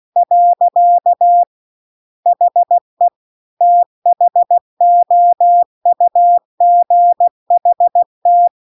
Morse code, heard as a keyed tone, about 16 wpm, 700 Hz. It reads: . HE THOUGHT
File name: CW-The-Big-Twoheaded-River-2-16WPM